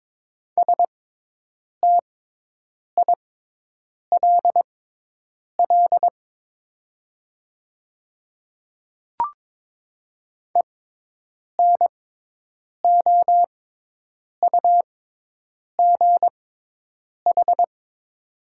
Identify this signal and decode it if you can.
Morse code, heard as a keyed tone, about 23 wpm, spaced as for 8 wpm, 700 Hz. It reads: STILL ENOUGH
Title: 6x Character Spacing - Top 500 Words - 22wpm